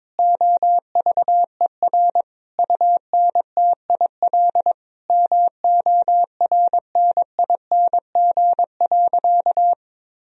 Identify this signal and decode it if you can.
Morse code, heard as a keyed tone, about 21 wpm, 700 Hz. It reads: OVER UNTIL MORNING.
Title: Sentences from Top 1000 Words - Review Part 1 - 22wpm